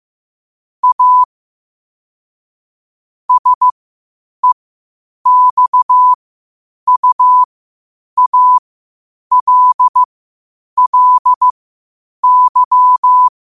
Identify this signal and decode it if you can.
Morse code, heard as a keyed tone, about 15 wpm, spaced as for 8 wpm, 1000 Hz. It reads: A SEXUALLY
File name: qotdfiles / 8wpm